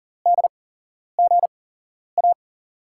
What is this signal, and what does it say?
Morse code, heard as a keyed tone, about 40 wpm, spaced as for 11 wpm, 700 Hz. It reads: DGA
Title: USA Call Signs - 6 Characters - 8x Character Spacing - 40wpm